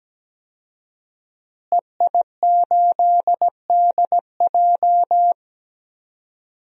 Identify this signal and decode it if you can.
Morse code, heard as a keyed tone, about 17 wpm, 700 Hz. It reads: EI8DJ
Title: DXCC-100 Call Signs - 5 Characters - Repeated 2x - 17wpm